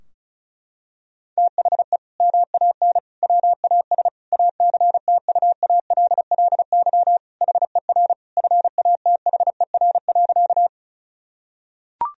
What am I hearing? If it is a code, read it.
Morse code, heard as a keyed tone, about 34 wpm, 700 Hz. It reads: THE MAN WAS ACTUALLY HER FATHER.
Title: Sentences from Top 1000 Words - Review Part 4 - 35wpm